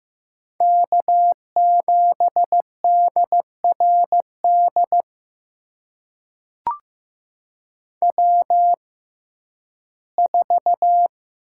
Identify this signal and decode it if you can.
Morse code, heard as a keyed tone, about 15 wpm, spaced as for 11 wpm, 700 Hz. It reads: K7DRD W4
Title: USA Call Signs - 5 Characters - 6x Character Spacing - 15wpm